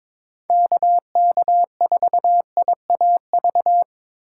Morse code, heard as a keyed tone, about 22 wpm, 700 Hz. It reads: KK4IAV
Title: USA Call Signs - 6 Characters - 6x Character Spacing - 22wpm